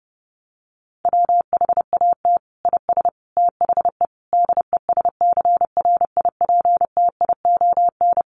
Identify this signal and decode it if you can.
Morse code, heard as a keyed tone, about 30 wpm, 700 Hz. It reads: WHAT IS THE DESCRIPTION
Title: Sentences from Top 1300 Words 30wpm